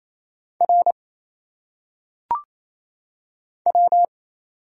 Morse code, heard as a keyed tone, about 27 wpm, 700 Hz. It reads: R W
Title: Single Letters - Phonetic Pronunciation 28wpm